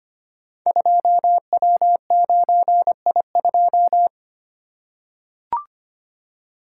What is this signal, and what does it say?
Morse code, heard as a keyed tone, about 25 wpm, 700 Hz. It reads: 2W9I2
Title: Code Groups - 5 Letter:Numbers - 25wpm